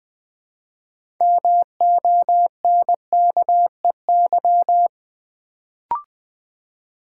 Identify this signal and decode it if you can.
Morse code, heard as a keyed tone, about 20 wpm, 700 Hz. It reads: MONKEY
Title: Build A Word - 3 Letters At A Time - 6-Letter Words - 20wpm